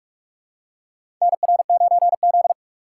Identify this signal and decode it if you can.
Morse code, heard as a keyed tone, about 46 wpm, 690 Hz. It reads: NR9Z
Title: USA Call Signs - 4 Characters - 3x Character Spacing - 45wpm